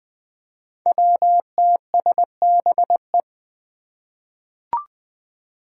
Morse code, heard as a keyed tone, about 20 wpm, 700 Hz. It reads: WTSBE
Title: Five Challenging Letter-Number Combinations 20wpm